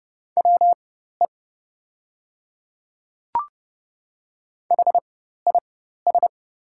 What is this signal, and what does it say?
Morse code, heard as a keyed tone, about 29 wpm, spaced as for 14 wpm, 700 Hz. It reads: WE HIS
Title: 4x Character Spacing - Top 200 Words - 30wpm